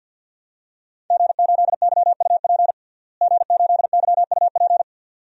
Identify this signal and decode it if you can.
Morse code, heard as a keyed tone, about 48 wpm, 700 Hz. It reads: G8YUP G8YUP
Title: DXCC-100 Call Signs - 5 Characters - Repeated 2x - 50wpm